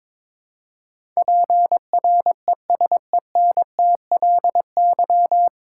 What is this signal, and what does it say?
Morse code, heard as a keyed tone, about 22 wpm, 700 Hz. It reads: PRESENTLY
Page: Morse Code Ninja practice 22wpm